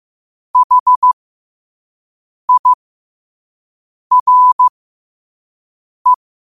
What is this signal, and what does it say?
Morse code, heard as a keyed tone, about 15 wpm, spaced as for 5 wpm, 1000 Hz. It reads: HIRE
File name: qotdfiles / 5wpm